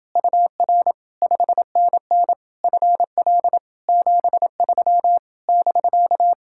Morse code, heard as a keyed tone, about 29 wpm, 700 Hz. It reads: UR 5NN FL 73 <BK>